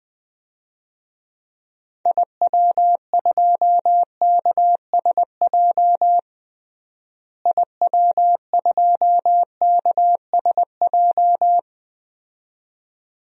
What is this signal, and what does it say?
Morse code, heard as a keyed tone, about 21 wpm, 700 Hz. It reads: IW2KSJ IW2KSJ
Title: DXCC-100 Call Signs - 6 Characters - Repeated 2x - 20wpm